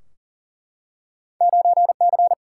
Morse code, heard as a keyed tone, about 42 wpm, 700 Hz. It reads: 9C
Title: Single-Letter Single-Number 40wpm